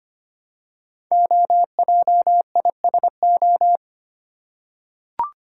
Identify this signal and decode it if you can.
Morse code, heard as a keyed tone, about 25 wpm, 700 Hz. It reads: OJISO
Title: Code Groups - 5 Letters - 25wpm